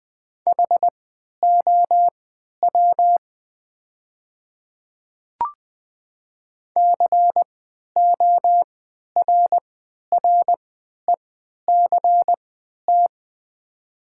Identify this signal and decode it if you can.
Morse code, heard as a keyed tone, about 20 wpm, spaced as for 11 wpm, 700 Hz. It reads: HOW CORRECT
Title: 3x Character Spacing - Top 500 Words - 20wpm